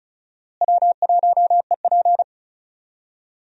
Morse code, heard as a keyed tone, about 36 wpm, 700 Hz. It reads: W1EP
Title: USA Call Signs - 4 Characters - Repeated 3x - 35wpm